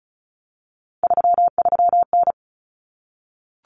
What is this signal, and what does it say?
Morse code, heard as a keyed tone, about 34 wpm, 700 Hz. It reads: 33N
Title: Common Cut Numbers - 3-Digit 35wpm